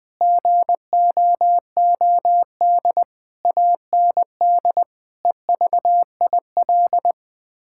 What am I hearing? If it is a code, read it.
Morse code, heard as a keyed tone, about 20 wpm, 700 Hz. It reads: GOOD AND EVIL
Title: Sets of 9 Words - 20wpm